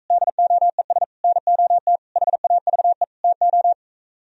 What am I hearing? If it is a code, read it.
Morse code, heard as a keyed tone, about 45 wpm, 700 Hz. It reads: DOES NOT HAVE TO